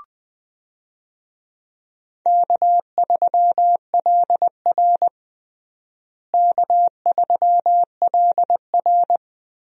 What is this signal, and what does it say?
Morse code, heard as a keyed tone, about 20 wpm, 700 Hz. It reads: K3LR K3LR
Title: USA Call Signs - 4 Characters - Repeated 3x - 20wpm